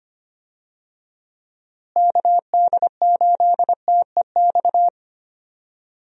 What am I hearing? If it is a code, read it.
Morse code, heard as a keyed tone, about 25 wpm, 700 Hz. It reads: KD8TEX
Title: USA Call Signs - 6 Characters - Repeated 3x - 25wpm